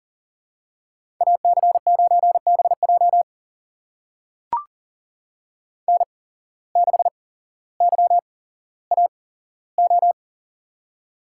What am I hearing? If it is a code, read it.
Morse code, heard as a keyed tone, about 42 wpm, spaced as for 17 wpm, 700 Hz. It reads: AC9BJ N6YAO
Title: USA Call Signs - 5 Characters - 8x Character Spacing - 40wpm